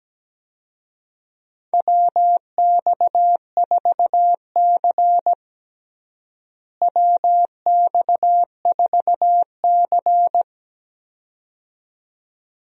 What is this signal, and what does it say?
Morse code, heard as a keyed tone, about 17 wpm, 700 Hz. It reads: WX4C WX4C
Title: USA Call Signs - 4 Characters - Repeated 2x - 17wpm